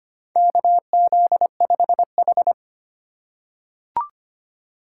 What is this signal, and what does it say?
Morse code, heard as a keyed tone, about 25 wpm, 700 Hz. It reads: KZ5H
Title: USA Call Signs - 4 Characters - 4x Character Spacing - 25wpm